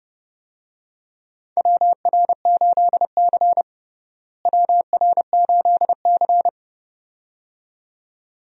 Morse code, heard as a keyed tone, about 30 wpm, 700 Hz. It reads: WR8C WR8C
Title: USA Call Signs - 4 Characters - Repeated 2x - 30wpm